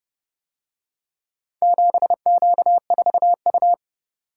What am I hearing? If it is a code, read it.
Morse code, heard as a keyed tone, about 25 wpm, 700 Hz. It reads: 7Q4U
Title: Speed-Racing - Four Letter-Number Combinations - 20wpm